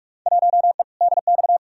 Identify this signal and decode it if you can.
Morse code, heard as a keyed tone, about 46 wpm, 700 Hz. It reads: 1E DX
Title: Field Day Exchanges Encore - 45wpm